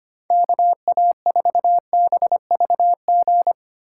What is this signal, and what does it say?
Morse code, heard as a keyed tone, about 25 wpm, 700 Hz. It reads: KA4BVG
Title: K1USN SST QSO Practice - 25wpm